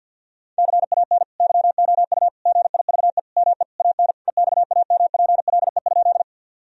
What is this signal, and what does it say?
Morse code, heard as a keyed tone, about 52 wpm, 700 Hz. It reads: CAN YOU GIVE ME AN EXAMPLE?
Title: Sentences from Top 1000 Words - Review Part 1 - 50wpm